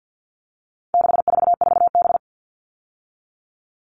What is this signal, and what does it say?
Morse code, heard as a keyed tone, about 50 wpm, 700 Hz. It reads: 644B
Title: All Cut Numbers - 4-Digit 50wpm